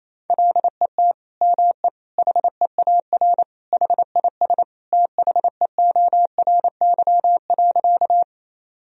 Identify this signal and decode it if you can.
Morse code, heard as a keyed tone, about 27 wpm, 700 Hz. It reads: LET ME HEAR HIS THEORY.
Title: Sentences from Top 1000 Words - Review Part 3 - 28wpm